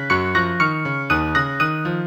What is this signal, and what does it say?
There is no code.